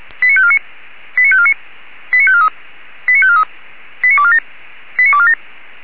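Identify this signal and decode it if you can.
Five-tone selective call (ZVEI), heard as quick runs of five tones, tones of about 70 ms. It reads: 89549 89549 89543 89543 89247 89247
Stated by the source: ZVEI five-tone selective call received on SDR.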